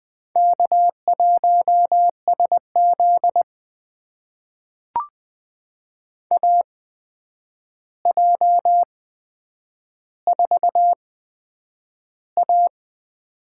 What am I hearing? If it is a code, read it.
Morse code, heard as a keyed tone, about 20 wpm, spaced as for 9 wpm, 700 Hz. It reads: K1SZ AJ4A
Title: USA Call Signs - 4 Characters - 8x Character Spacing - 20wpm